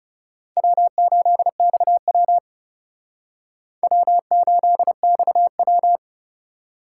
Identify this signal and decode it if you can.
Morse code, heard as a keyed tone, about 37 wpm, 700 Hz. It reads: W8XW W8XW
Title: Speed Racing - USA 1x2 Call Signs 30wpm